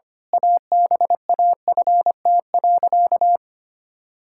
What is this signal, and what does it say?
Morse code, heard as a keyed tone, about 25 wpm, 700 Hz. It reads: ABAFT.
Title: Pangram Challenge v2 - 25wpm